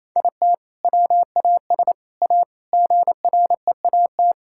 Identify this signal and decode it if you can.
Morse code, heard as a keyed tone, about 28 wpm, 700 Hz. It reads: IT WAS A GREAT
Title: Sentences from Top 1000 Words - Review Part 6 - 28wpm